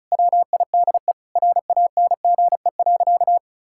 Morse code, heard as a keyed tone, about 36 wpm, 700 Hz. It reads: WIDE RANGE.
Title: Sentences from Top 1000 Words - Review Part 6 - 35wpm